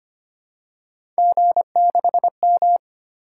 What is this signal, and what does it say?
Morse code, heard as a keyed tone, about 25 wpm, 700 Hz. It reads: G6M
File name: Three Letter-Number Combinations - 25wpm